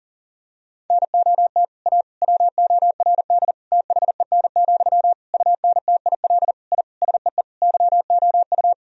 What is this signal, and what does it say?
Morse code, heard as a keyed tone, about 40 wpm, 700 Hz. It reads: NOT A WORD THEN, UNTIL I SEE YOU
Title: Sentences from Top 1000 Words - Review Part 1 - 40wpm